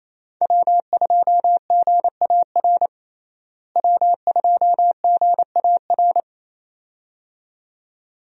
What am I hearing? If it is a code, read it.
Morse code, heard as a keyed tone, about 28 wpm, 700 Hz. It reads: W2GAR W2GAR
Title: USA Call Signs - 5 Characters - Repeated 2x - 28wpm